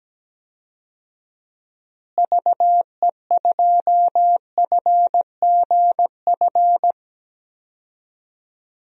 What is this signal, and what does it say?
Morse code, heard as a keyed tone, about 17 wpm, 700 Hz. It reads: VE2FGF
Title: DXCC-100 Call Signs - 6 Characters - Repeated 2x - 17wpm